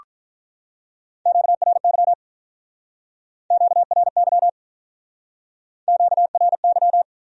Morse code, heard as a keyed tone, about 46 wpm, 700 Hz. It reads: QRY QRY QRY